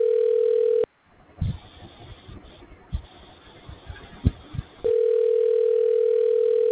Ringback tone (North America), 440 + 480 Hz.